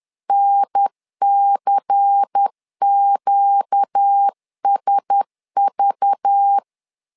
Morse code, heard as a keyed tone, about 10 wpm, 800 Hz. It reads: NCQSV
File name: LEZIONI / 5 LBVC / 5 LBVC 10wpm 2349